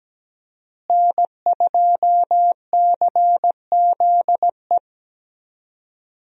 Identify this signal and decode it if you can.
Morse code, heard as a keyed tone, about 17 wpm, 700 Hz. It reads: N2CZE
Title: USA Call Signs - 5 Characters - Repeated 2x - 17wpm